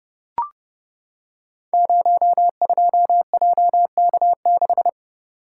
Morse code, heard as a keyed tone, about 30 wpm, 700 Hz. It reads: 02JK6
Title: Code Groups - 5 Letter:Numbers - 30wpm